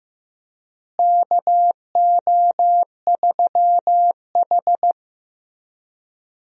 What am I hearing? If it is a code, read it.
Morse code, heard as a keyed tone, about 15 wpm, 700 Hz. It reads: KO3H
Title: USA Call Signs - 4 Characters - Repeated 3x - 15wpm